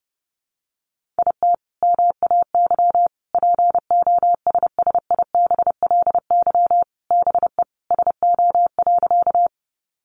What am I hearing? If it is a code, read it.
Morse code, heard as a keyed tone, about 30 wpm, 700 Hz. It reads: IT MAY POSSIBLY BE SO.
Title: Sentences from Top 1300 Words 30wpm